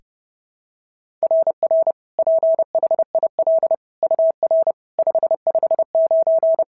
Morse code, heard as a keyed tone, about 32 wpm, 660 Hz. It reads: RR PHIL UR 559
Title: SOTA QSO Practice 30wpm